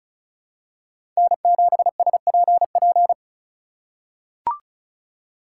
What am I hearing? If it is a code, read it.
Morse code, heard as a keyed tone, about 34 wpm, 700 Hz. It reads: N7SPP
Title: USA Call Signs - 5 Characters - 4x Character Spacing - 35wpm